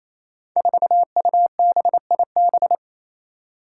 Morse code, heard as a keyed tone, about 29 wpm, 700 Hz. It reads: 4UBIB
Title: Five Challenging Letter-Number Combinations 28wpm